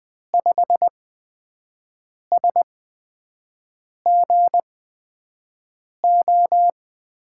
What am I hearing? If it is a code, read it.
Morse code, heard as a keyed tone, about 20 wpm, spaced as for 5 wpm, 700 Hz. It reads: 5SGO
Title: USA Call Signs - 6 Characters - 8x Character Spacing - 20wpm